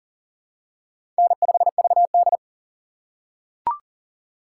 Morse code, heard as a keyed tone, about 40 wpm, 700 Hz. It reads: N5VD
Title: USA Call Signs - 4 Characters - 6x Character Spacing - 40wpm